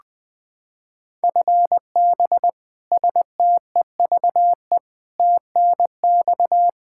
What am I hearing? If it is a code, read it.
Morse code, heard as a keyed tone, about 20 wpm, 700 Hz. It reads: FB STEVE TNX
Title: Practice QSO - Exchange 3 - More Info - 20wpm